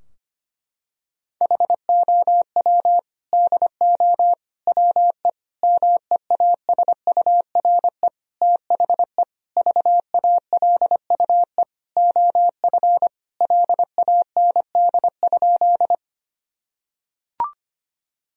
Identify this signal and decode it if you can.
Morse code, heard as a keyed tone, about 25 wpm, 700 Hz. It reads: HOW DO WE MEASURE THE VALUE OF LAND?
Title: Sentences from Top 1000 Words - Review Part 3 - 25wpm